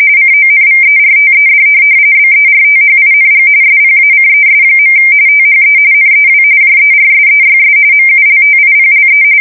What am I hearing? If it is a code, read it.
RTTY, about 45.45 baud. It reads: CONVERTED BERTHING SPACES. THE NEW JERSEYS MULTIPLE
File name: fsk170 / USS-New-Jersey